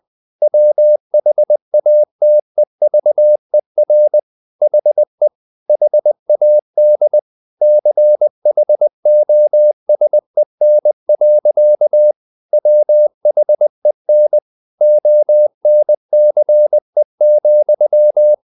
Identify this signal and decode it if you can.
Morse code, heard as a keyed tone, about 20 wpm, 600 Hz. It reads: WHATEVER HE HAD CHOSEN. WHEN ONCE,
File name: the old man and the sea / the old man and the sea part 1 20wpm